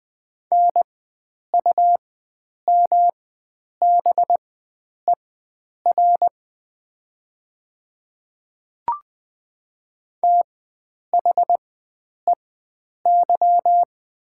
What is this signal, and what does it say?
Morse code, heard as a keyed tone, about 20 wpm, spaced as for 9 wpm, 700 Hz. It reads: NUMBER THEY
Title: 4x Character Spacing - Top 100 Words - 20wpm